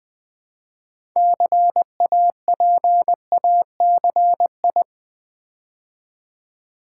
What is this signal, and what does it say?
Morse code, heard as a keyed tone, about 20 wpm, 700 Hz. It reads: CAPACI
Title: Build A Word - 2 Letters At A Time - 8-Letter Words - 20wpm